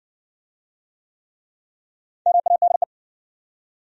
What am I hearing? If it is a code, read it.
Morse code, heard as a keyed tone, about 60 wpm, 700 Hz. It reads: MADE